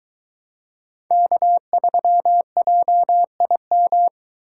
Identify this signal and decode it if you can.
Morse code, heard as a keyed tone, about 22 wpm, 700 Hz. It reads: K3JIM